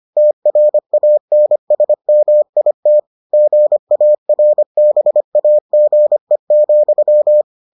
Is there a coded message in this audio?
Yes: Morse code from 0.2 to 7.4 s, heard as a keyed tone, about 25 wpm, 600 Hz. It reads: TRANSMIT GARBAGE,